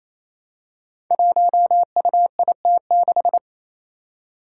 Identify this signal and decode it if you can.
Morse code, heard as a keyed tone, about 27 wpm, 700 Hz. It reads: 1UIT6
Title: Five Challenging Letter-Number Combinations 28wpm